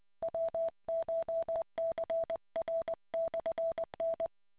Morse code, heard as a keyed tone, about 20 wpm, 660 Hz. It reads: WCHCR/N